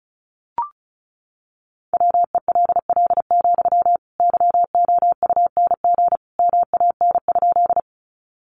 Morse code, heard as a keyed tone, about 34 wpm, 700 Hz. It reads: WELL, YOUNG MAN?
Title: Sentences from Top 400 Words 35wpm